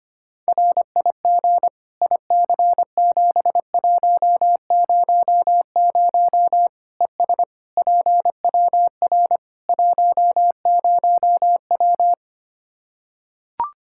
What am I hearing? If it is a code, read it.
Morse code, heard as a keyed tone, about 25 wpm, 700 Hz. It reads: RIG IC7100 ES PWR 10W
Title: K1USN SST QSO Practice 25wpm